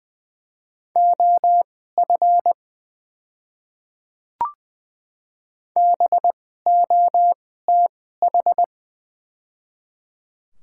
Morse code, heard as a keyed tone, about 20 wpm, spaced as for 14 wpm, 700 Hz. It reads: OF BOTH
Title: 2x Character Spacing - Top 500 Words - 20wpm